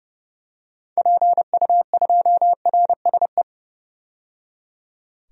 Morse code, heard as a keyed tone, about 31 wpm, 700 Hz. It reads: PU2RSE
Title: DXCC-100 Call Signs - 6 Characters - 30wpm